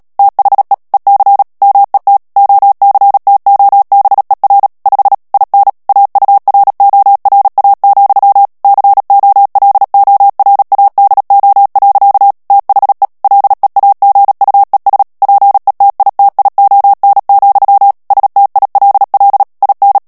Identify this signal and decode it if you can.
Morse code, heard as a keyed tone, about 37 wpm, 800 Hz. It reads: THE EC MET OCTOBER 5 IN AURORA, COLORADO. THE LEAGUES PETITION, STILL IN
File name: morse mp3 / ANGLAIS texte clair 3 vitesse 35 mots minute